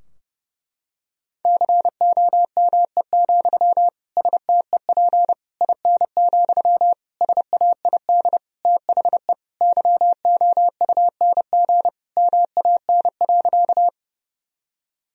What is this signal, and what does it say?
Morse code, heard as a keyed tone, about 30 wpm, 700 Hz. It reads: COME, STEP IN, SAID THE YOUNG MAN.
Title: Sentences from Top 1000 Words - Review Part 2 - 30wpm